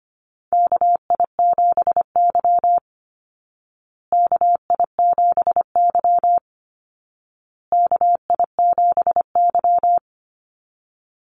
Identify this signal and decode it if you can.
Morse code, heard as a keyed tone, about 25 wpm, 700 Hz. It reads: KI7Y KI7Y KI7Y
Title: Call Signs Repeated 2x - 25wpm